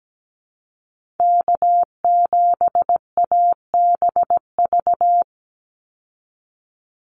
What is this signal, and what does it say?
Morse code, heard as a keyed tone, about 17 wpm, 700 Hz. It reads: K7ABV
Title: Call Signs Repeated 3x - 17wpm